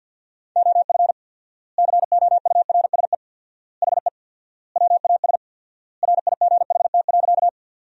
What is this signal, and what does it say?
Morse code, heard as a keyed tone, about 50 wpm, 700 Hz. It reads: OF COURSE HE WAS RIGHT.
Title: 4x Word-Spacing - Sentences from Top 500 Words - 50wpm